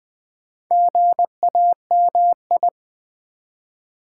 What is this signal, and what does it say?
Morse code, heard as a keyed tone, about 20 wpm, 700 Hz. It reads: GAMI
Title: Build A Word - 2 Letters At A Time - 6-Letter Words - 20wpm